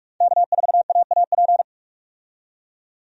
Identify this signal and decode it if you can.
Morse code, heard as a keyed tone, about 46 wpm, 700 Hz. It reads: K4AAP
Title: USA Call Signs - 5 Characters - 45wpm